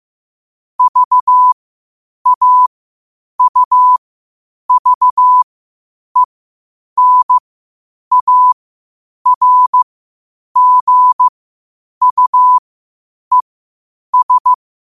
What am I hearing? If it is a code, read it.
Morse code, heard as a keyed tone, about 15 wpm, spaced as for 8 wpm, 1000 Hz. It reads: VAUVENARGUES
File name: qotdfiles / 8wpm